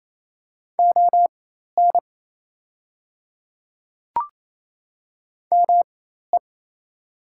Morse code, heard as a keyed tone, about 28 wpm, spaced as for 13 wpm, 700 Hz. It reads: ON ME
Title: 4x Character Spacing - Top 200 Words - 28wpm